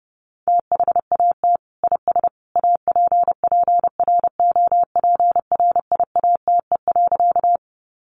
Morse code, heard as a keyed tone, about 30 wpm, 700 Hz. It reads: THAT IS APPROPRIATE.
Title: Sentences from Top 900 Words 30wpm